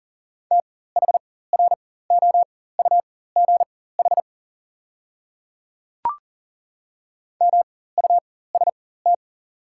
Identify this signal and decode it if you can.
Morse code, heard as a keyed tone, about 40 wpm, spaced as for 19 wpm, 690 Hz. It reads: THROUGH MUST
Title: 4x Character Spacing - Top 200 Words - 40wpm